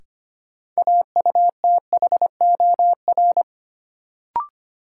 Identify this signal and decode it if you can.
Morse code, heard as a keyed tone, about 25 wpm, 700 Hz. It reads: AUTHOR
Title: Sentences from Top 1200 Words - New Word Drill - 25wpm